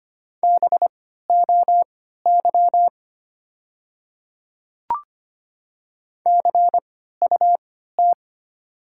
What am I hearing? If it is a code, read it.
Morse code, heard as a keyed tone, about 25 wpm, spaced as for 14 wpm, 700 Hz. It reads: BOY CUT
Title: 3x Character Spacing - Top 500 Words - 25wpm